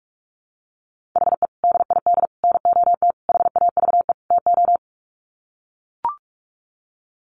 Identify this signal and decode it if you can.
Morse code, heard as a keyed tone, about 44 wpm, 700 Hz. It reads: HE DID NOT HAVE TO
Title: Sets of 5 Words 45wpm